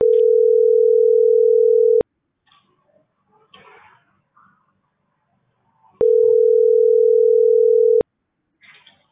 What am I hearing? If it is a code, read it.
Ringback tone (North America), 440 + 480 Hz.